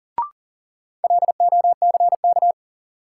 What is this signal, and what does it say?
Morse code, heard as a keyed tone, about 40 wpm, 700 Hz. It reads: LOCK
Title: Sentences from Top 1300 Words - New Word Drill Alt - 40wpm